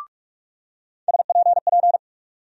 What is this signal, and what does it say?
Morse code, heard as a keyed tone, about 45 wpm, 700 Hz. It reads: SPP
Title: USA Call Sign Postfixes - 3 Letters 45wpm